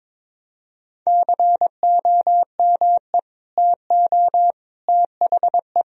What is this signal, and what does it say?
Morse code, heard as a keyed tone, about 21 wpm, 700 Hz. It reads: COME TO THE
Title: Sentences from Top 1000 Words - Review Part 2 - 22wpm